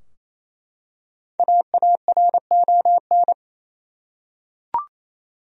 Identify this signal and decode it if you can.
Morse code, heard as a keyed tone, about 28 wpm, 700 Hz. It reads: AARON
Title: USA Common Names - 28wpm